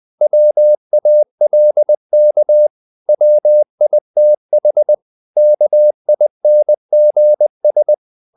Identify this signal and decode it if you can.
Morse code, heard as a keyed tone, about 20 wpm, 600 Hz. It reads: WALK WITH KINGS
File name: CW / Practice / If20